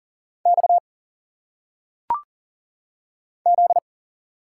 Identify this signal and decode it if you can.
Morse code, heard as a keyed tone, about 40 wpm, 700 Hz. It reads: X Z
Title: Single Letters - Phonetic Pronunciation 40wpm